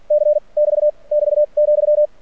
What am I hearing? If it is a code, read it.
Morse code, heard as a keyed tone, about 46 wpm, 600 Hz. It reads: C <BT> <BT> ,
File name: archived / CRSnet